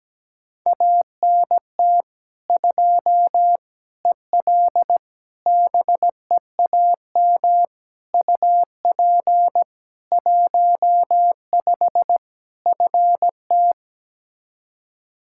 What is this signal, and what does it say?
Morse code, heard as a keyed tone, about 17 wpm, 700 Hz. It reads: ANT 2 EL BEAM UP 15 FT
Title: Practice QSO - Exchange 2 - Info - 17wpm